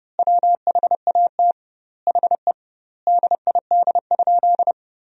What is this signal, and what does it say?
Morse code, heard as a keyed tone, about 30 wpm, 700 Hz. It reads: WHAT HE DID?
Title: 2x Word-Spacing - Sentences from Top 100 Words - 30wpm